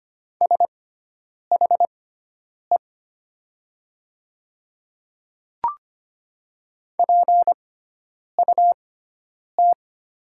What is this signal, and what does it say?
Morse code, heard as a keyed tone, about 25 wpm, spaced as for 9 wpm, 700 Hz. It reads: SHE PUT
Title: 6x Character Spacing - Top 300 Words - 25wpm